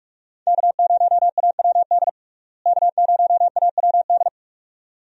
Morse code, heard as a keyed tone, about 46 wpm, 700 Hz. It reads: K0AWD K0AWD
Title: USA Call Signs - 5 Characters - Repeated 2x - 45wpm